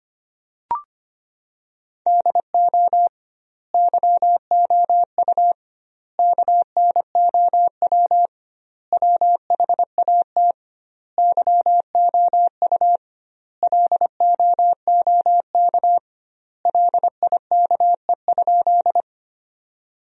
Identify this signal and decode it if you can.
Morse code, heard as a keyed tone, about 25 wpm, 700 Hz. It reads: DO YOU KNOW WHAT YOU LOOK LIKE?
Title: 2x Word-Spacing - Sentences from Top 100 Words - 25wpm